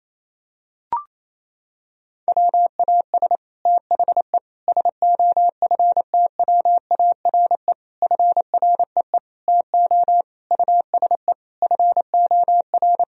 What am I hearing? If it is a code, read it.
Morse code, heard as a keyed tone, about 28 wpm, 700 Hz. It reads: WAS THE SOFTWARE FREE TO USE FOR
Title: Sentences from Top 1000 Words - Review Part 3 - 28wpm